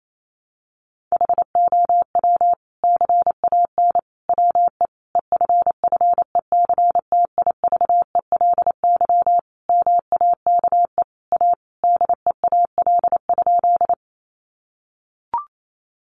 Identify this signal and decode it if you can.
Morse code, heard as a keyed tone, about 28 wpm, 700 Hz. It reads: HOW CAN WE EFFECTIVELY MAKE A DEAL?
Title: Sentences from Top 1400 Words 28wpm